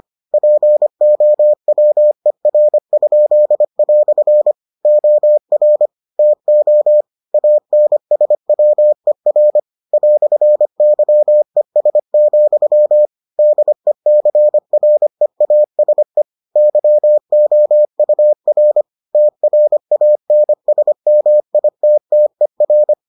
Morse code, heard as a keyed tone, about 25 wpm, 600 Hz. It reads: POWER?" OR TO ANSWER "YES, DECREASE YOUR TRANSMITTER